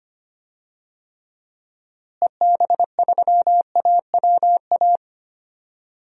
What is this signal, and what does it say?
Morse code, heard as a keyed tone, about 25 wpm, 700 Hz. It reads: EB3AWA
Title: DXCC-100 Call Signs - 6 Characters - Repeated 2x - 25wpm